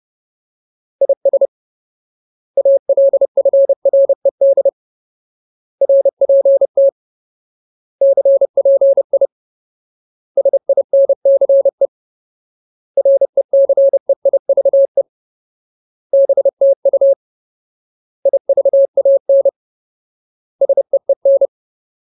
Morse code, heard as a keyed tone, about 30 wpm, 550 Hz. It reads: IS ALFRED RPT CPI SINCE RECEIVE BTU IVAN SEEN
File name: Ultimate Word List / 30WPM Ultimate